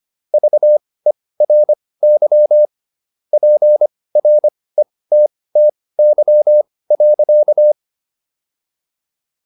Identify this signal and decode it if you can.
Morse code, heard as a keyed tone, about 25 wpm, spaced as for 18 wpm, 600 Hz. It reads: VERY PRETTY.